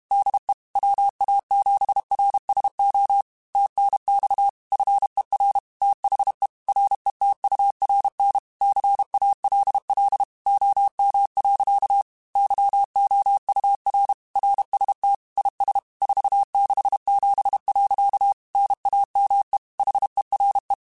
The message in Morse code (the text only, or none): DE WA7RSO TNX FER THE RETURN CALL OM. YOUR RST IS 467. NAME HERE